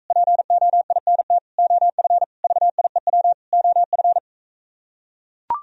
Morse code, heard as a keyed tone, about 43 wpm, 700 Hz. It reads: POINT OF VIEW OF